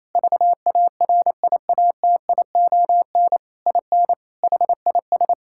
Morse code, heard as a keyed tone, about 28 wpm, 700 Hz. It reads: VARIATION IN HIS
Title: Sentences from Top 1600 Words 28wpm